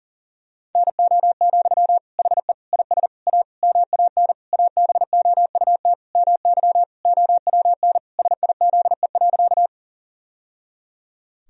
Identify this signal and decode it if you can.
Morse code, heard as a keyed tone, about 40 wpm, 700 Hz. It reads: NO, HE IS A MAN ABOUT MY OWN SIZE.
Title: Sentences from Top 1000 Words - Review Part 2 - 40wpm